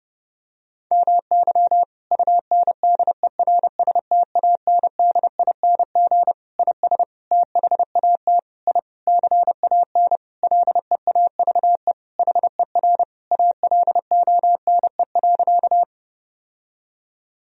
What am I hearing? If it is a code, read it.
Morse code, heard as a keyed tone, about 30 wpm, 700 Hz. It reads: MY UNDERSTANDING IS THAT I CAN LEAVE HER ALONE.
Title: Sentences from Top 1000 Words - Review Part 3 - 30wpm